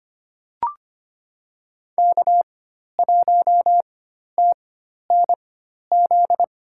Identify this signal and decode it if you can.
Morse code, heard as a keyed tone, about 25 wpm, spaced as for 12 wpm, 700 Hz. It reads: K1TNZ
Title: USA Call Signs - 5 Characters - 4x Character Spacing - 25wpm